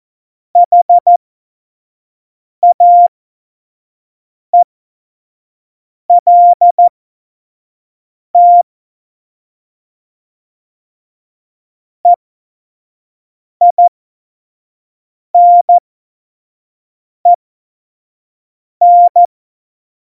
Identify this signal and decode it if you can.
Morse code, heard as a keyed tone, about 14 wpm, spaced as for 5 wpm, 700 Hz. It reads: HAELT EINEN
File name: Morsekurs / Witze / Witz 14. August 2018, 25 F70